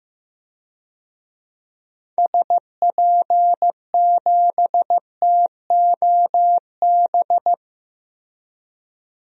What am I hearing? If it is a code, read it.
Morse code, heard as a keyed tone, about 16 wpm, 700 Hz. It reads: SP7TOB
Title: DXCC-100 Call Signs - 6 Characters - Repeated 3x - 15wpm